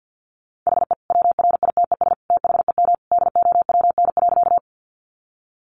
Morse code, heard as a keyed tone, about 50 wpm, 700 Hz. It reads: HE WRITES THEM DOWN.
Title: Additional Easy Phrases 50wpm